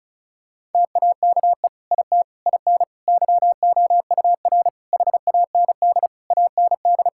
Morse code, heard as a keyed tone, about 36 wpm, 700 Hz. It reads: TAKE IT IN YOUR HAND AND
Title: Sentences from Top 1600 Words 35wpm